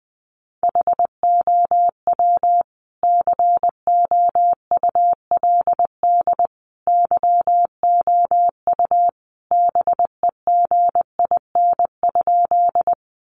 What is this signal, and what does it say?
Morse code, heard as a keyed tone, about 20 wpm, 700 Hz. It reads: HOW COULD YOU BEGIN?
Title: Sentences from Top 300 Words 20wpm